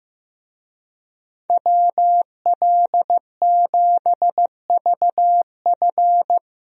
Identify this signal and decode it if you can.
Morse code, heard as a keyed tone, about 15 wpm, 700 Hz. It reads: WL7VF
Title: USA Call Signs - 5 Characters - 15wpm